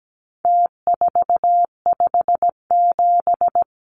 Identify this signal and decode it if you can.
Morse code, heard as a keyed tone, about 17 wpm, 700 Hz. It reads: T457
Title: Numbers - Contest Style - 4-Digit 17wpm